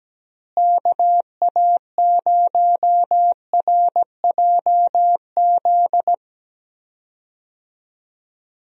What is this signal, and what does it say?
Morse code, heard as a keyed tone, about 17 wpm, 700 Hz. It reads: KA0RJZ
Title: USA Call Signs - 6 Characters - Repeated 3x - 17wpm